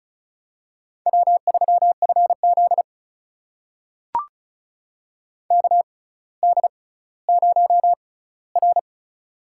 Morse code, heard as a keyed tone, about 36 wpm, spaced as for 18 wpm, 700 Hz. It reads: W3FZ KD0R
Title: USA Call Signs - 4 Characters - 6x Character Spacing - 35wpm